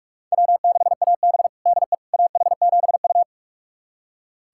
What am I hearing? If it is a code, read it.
Morse code, heard as a keyed tone, about 46 wpm, 700 Hz. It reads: W6AB DE AH7U
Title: USA Callsign DE Callsign - 4 Characters - 45wpm